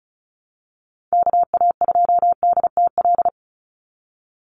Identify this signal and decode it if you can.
Morse code, heard as a keyed tone, about 34 wpm, 700 Hz. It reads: KA2DTL
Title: USA Call Signs 35wpm